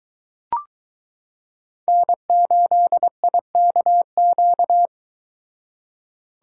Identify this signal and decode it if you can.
Morse code, heard as a keyed tone, about 22 wpm, 700 Hz. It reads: N8IKQ